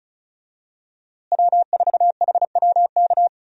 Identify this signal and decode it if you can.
Morse code, heard as a keyed tone, about 36 wpm, 700 Hz. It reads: W4HWK
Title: USA Call Signs - 5 Characters - Repeated 2x - 35wpm